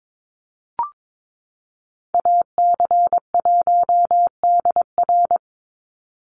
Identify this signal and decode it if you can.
Morse code, heard as a keyed tone, about 22 wpm, 700 Hz. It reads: AC1DR
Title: USA Call Signs 22wpm